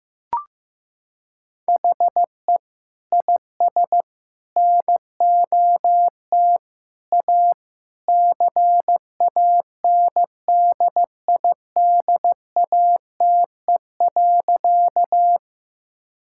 Morse code, heard as a keyed tone, about 16 wpm, 700 Hz. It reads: HE IS NOT A CANDIDATE.
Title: Sentences from Top 1600 Words 15wpm